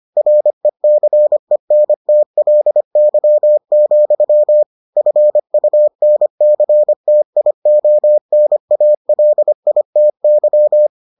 Morse code, heard as a keyed tone, about 25 wpm, 600 Hz. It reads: RECENTLY, FUNCTIONALITY